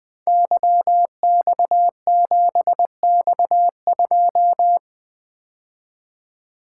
Morse code, heard as a keyed tone, about 20 wpm, 700 Hz. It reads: YX7X2
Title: Code Groups - 5 Letter:Numbers - 20wpm